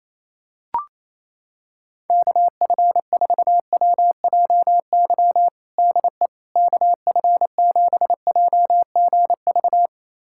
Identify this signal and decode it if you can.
Morse code, heard as a keyed tone, about 28 wpm, 700 Hz. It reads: KF4WJY DE KF7JGV
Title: USA Callsign DE Callsign 28wpm